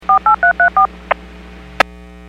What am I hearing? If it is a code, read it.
DTMF, heard as telephone touch tones, tones of about 110 ms. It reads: 48334